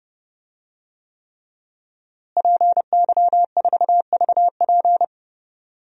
Morse code, heard as a keyed tone, about 30 wpm, 700 Hz. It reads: PY4VP
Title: DXCC-100 Call Signs - 5 Characters - Repeated 3x - 30wpm